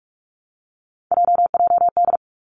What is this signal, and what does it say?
Morse code, heard as a keyed tone, about 46 wpm, 700 Hz. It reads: JJD
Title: Three Challenging Letter-Number Combinations 45wpm